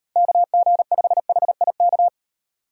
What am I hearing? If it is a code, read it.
Morse code, heard as a keyed tone, about 35 wpm, 700 Hz. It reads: KG5HIK